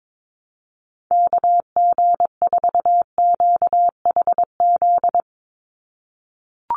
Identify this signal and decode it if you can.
Morse code, heard as a keyed tone, about 22 wpm, 700 Hz. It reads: KG4QHZ
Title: USA Call Signs 22wpm